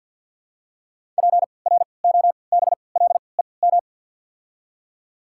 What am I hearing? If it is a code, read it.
Morse code, heard as a keyed tone, about 53 wpm, spaced as for 29 wpm, 700 Hz. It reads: PROBLEM
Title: 3x Character Spacing - Top 500 Words - 50wpm